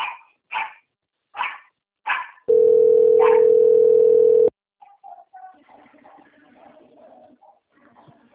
Ringback tone (North America), 440 + 480 Hz.
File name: out- - - - - .17271